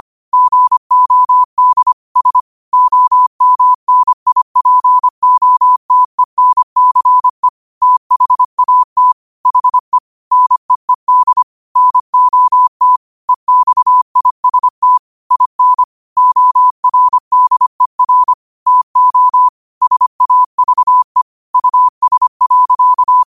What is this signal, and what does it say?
Morse code, heard as a keyed tone, about 25 wpm, 1000 Hz. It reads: GOD S OMNIPOTENCE THAT HE NEED NOT EXIST IN ORDER TO SAVE US.